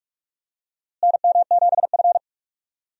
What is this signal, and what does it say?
Morse code, heard as a keyed tone, about 43 wpm, 700 Hz. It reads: NM7F
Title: USA Call Signs - 4 Characters - Repeated 3x - 45wpm